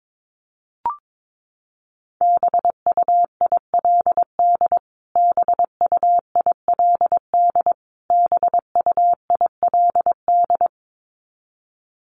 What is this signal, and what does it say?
Morse code, heard as a keyed tone, about 22 wpm, 700 Hz. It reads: BUILD BUILD BUILD
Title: Top 200 Words Repeated 22wpm